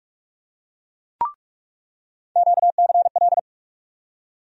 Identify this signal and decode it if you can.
Morse code, heard as a keyed tone, about 47 wpm, 700 Hz. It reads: QCL
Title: USA Call Sign Postfixes - 3 Letters 45wpm